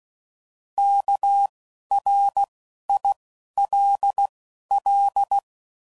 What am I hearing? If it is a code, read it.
Morse code, heard as a keyed tone, about 16 wpm, 800 Hz. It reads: K R I L L
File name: Prefixes / By First Letter / Ks